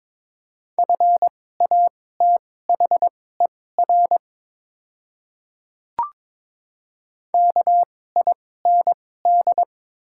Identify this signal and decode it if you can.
Morse code, heard as a keyed tone, about 22 wpm, spaced as for 16 wpm, 690 Hz. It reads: FATHER KIND
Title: 2x Character Spacing - Top 500 Words - 22wpm